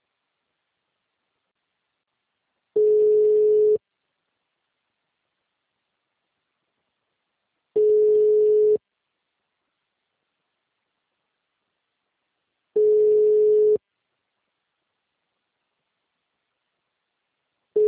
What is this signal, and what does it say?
Ringback tone (Europe, 425 Hz).